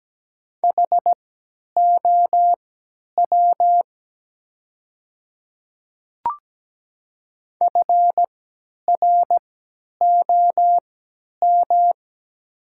Morse code, heard as a keyed tone, about 17 wpm, spaced as for 10 wpm, 700 Hz. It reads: HOW FROM
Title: 3x Character Spacing - Top 100 Words - 17wpm